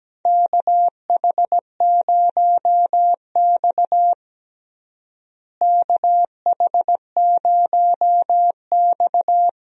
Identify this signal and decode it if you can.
Morse code, heard as a keyed tone, about 17 wpm, 700 Hz. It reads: KH0X KH0X
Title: USA Call Signs - 4 Characters - Repeated 2x - 17wpm